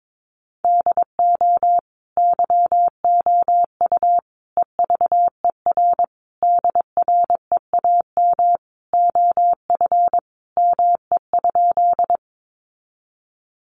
Morse code, heard as a keyed tone, about 22 wpm, 700 Hz. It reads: DO YOU EVER DREAM OF ME?
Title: Sentences from Top 1500 Words 22wpm